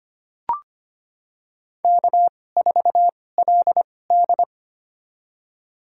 Morse code, heard as a keyed tone, about 25 wpm, spaced as for 18 wpm, 700 Hz. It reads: K4LD
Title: USA Call Signs - 4 Characters - 2x Character Spacing - 25wpm